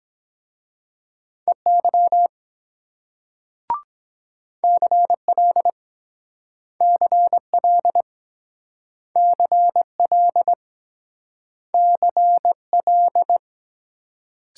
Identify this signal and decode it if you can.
Morse code, heard as a keyed tone, about 23 wpm, 700 Hz. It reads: EY CL CL CL CL